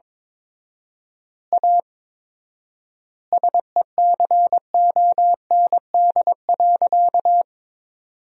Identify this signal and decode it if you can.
Morse code, heard as a keyed tone, about 22 wpm, 700 Hz. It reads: A SECOND.
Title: 4x Word-Spacing - Sentences from Top 500 Words - 22wpm